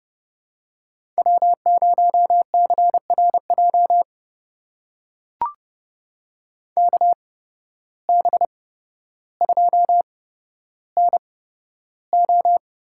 Morse code, heard as a keyed tone, about 30 wpm, spaced as for 13 wpm, 700 Hz. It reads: W0CRJ KB2NO
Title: USA Call Signs - 8x Character Spacing - 30wpm